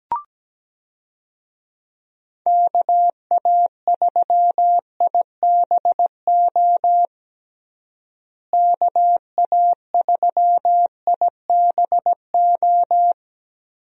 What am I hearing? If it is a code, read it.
Morse code, heard as a keyed tone, about 17 wpm, 700 Hz. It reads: KA3IBO KA3IBO
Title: USA Call Signs - 6 Characters - Repeated 2x - 17wpm